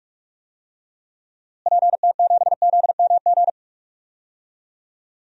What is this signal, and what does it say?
Morse code, heard as a keyed tone, about 46 wpm, 700 Hz. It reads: PT7ZMG
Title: DXCC-100 Call Signs - 6 Characters - Repeated 2x - 45wpm